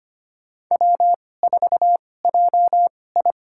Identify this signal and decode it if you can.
Morse code, heard as a keyed tone, about 25 wpm, spaced as for 18 wpm, 700 Hz. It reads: W4JI
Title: USA Call Signs - 2x Character Spacing - 25wpm